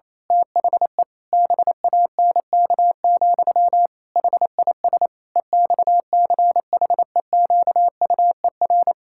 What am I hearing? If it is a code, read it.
Morse code, heard as a keyed tone, about 28 wpm, 700 Hz. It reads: THE BANK, HIS EXCHEQUER
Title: Pangram Challenge v2 - 28wpm